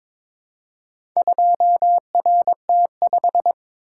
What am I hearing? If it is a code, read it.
Morse code, heard as a keyed tone, about 22 wpm, 700 Hz. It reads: 2RT5
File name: practice / 22wpm / Four Letter-Number Combinations - 22wpm